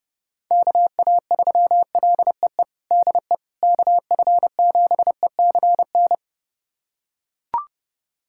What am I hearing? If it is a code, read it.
Morse code, heard as a keyed tone, about 32 wpm, 700 Hz. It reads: KA3LEE DE KF7ECN
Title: USA Callsign DE Callsign - 6 Characters - 30wpm